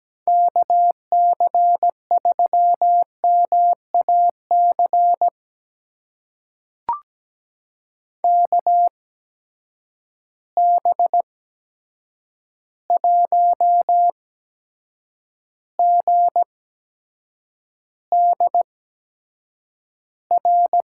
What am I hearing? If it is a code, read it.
Morse code, heard as a keyed tone, about 17 wpm, spaced as for 7 wpm, 700 Hz. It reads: KC3MAC KB1GDR
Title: USA Call Signs - 8x Character Spacing - 17wpm